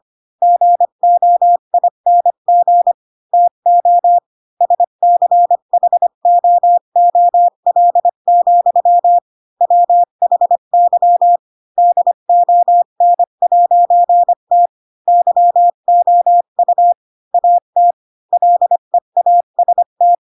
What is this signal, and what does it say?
Morse code, heard as a keyed tone, about 25 wpm, 700 Hz. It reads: GOING TO SCHOOL, WHY DON'T YOU AT LEAST
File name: Pinocchio / Pinocchio ch4 20 25 ik1hge